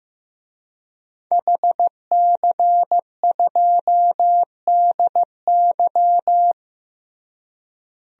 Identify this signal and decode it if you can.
Morse code, heard as a keyed tone, about 16 wpm, 700 Hz. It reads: HC2DY
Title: DXCC-100 Call Signs - 5 Characters - Repeated 3x - 15wpm